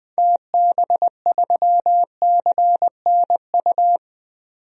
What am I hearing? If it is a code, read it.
Morse code, heard as a keyed tone, about 20 wpm, 700 Hz. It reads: TB3CNU
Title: DXCC-100 Call Signs - 6 Characters - 20wpm